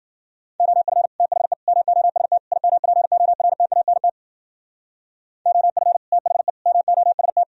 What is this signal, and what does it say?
Morse code, heard as a keyed tone, about 48 wpm, 700 Hz. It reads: OF THE MOST IMPORTANT OF THE MOST
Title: Speed-Racing - Sets of 4 Words 40wpm